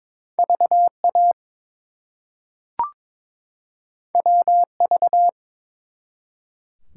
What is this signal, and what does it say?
Morse code, heard as a keyed tone, about 22 wpm, 700 Hz. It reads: VA WV
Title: US States and Canadian Provinces 22wpm